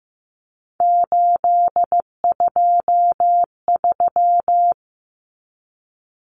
Morse code, heard as a keyed tone, about 15 wpm, 700 Hz. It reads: 823
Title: Common Cut Numbers - 3-Digit 15wpm